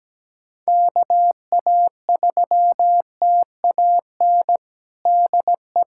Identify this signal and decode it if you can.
Morse code, heard as a keyed tone, about 17 wpm, 700 Hz. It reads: KA3TAN DE
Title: USA Callsign DE Callsign - 6 Characters - 17wpm